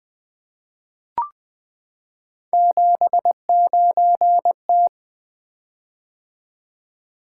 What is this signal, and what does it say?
Morse code, heard as a keyed tone, about 20 wpm, 700 Hz. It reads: 79T
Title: Three Letter-Number Combinations - 20wpm